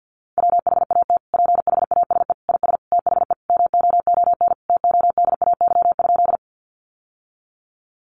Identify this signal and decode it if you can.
Morse code, heard as a keyed tone, about 50 wpm, 700 Hz. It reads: WHAT PHASE IS THE MOON TODAY?
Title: Sentences from Top 1200 Words 50wpm